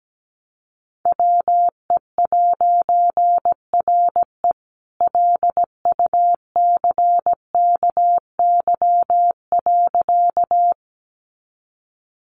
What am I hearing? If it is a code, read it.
Morse code, heard as a keyed tone, about 17 wpm, 700 Hz. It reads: WE'RE LUCKY.
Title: A Sound Of Thunder 17wpm